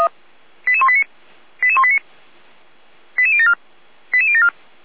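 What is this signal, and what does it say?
Five-tone selective call (ZVEI), heard as quick runs of five tones, tones of about 70 ms. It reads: 88189 88189 88074 88074
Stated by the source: ZVEI five-tone selective call received with an SDR.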